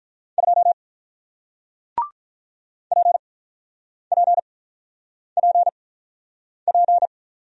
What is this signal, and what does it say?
Morse code, heard as a keyed tone, about 48 wpm, spaced as for 9 wpm, 700 Hz. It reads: 2 PPPP